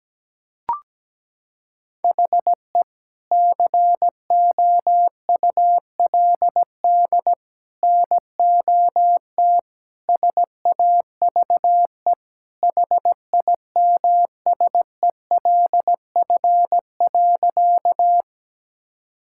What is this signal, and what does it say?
Morse code, heard as a keyed tone, about 17 wpm, 700 Hz. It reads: HE COULD NOT SAVE HIMSELF.
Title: Sentences from Top 1600 Words 17wpm